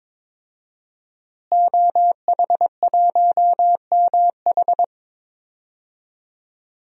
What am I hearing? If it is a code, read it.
Morse code, heard as a keyed tone, about 22 wpm, 700 Hz. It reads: OH1MH
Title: DXCC-100 Call Signs - 5 Characters - Repeated 2x - 22wpm